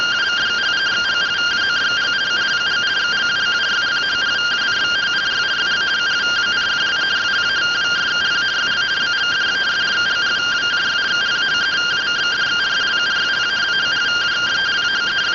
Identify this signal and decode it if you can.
Two-tone teleprinter signal, about 170 Hz shift.